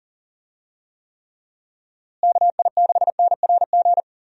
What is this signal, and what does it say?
Morse code, heard as a keyed tone, about 37 wpm, 700 Hz. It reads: KI6NRG
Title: USA Call Signs - 6 Characters - Repeated 3x - 40wpm